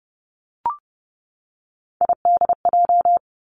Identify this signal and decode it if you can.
Morse code, heard as a keyed tone, about 30 wpm, 700 Hz. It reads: IDJ
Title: Three Challenging Letter-Number Combinations 30wpm